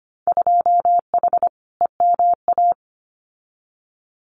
Morse code, heard as a keyed tone, about 25 wpm, 690 Hz. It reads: 2H EMA
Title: Winter Field Day Exchanges 25wpm